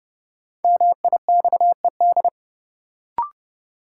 Morse code, heard as a keyed tone, about 30 wpm, 700 Hz. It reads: MIXED
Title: Sentences from Top 1500 Words - New Word Drill - 30wpm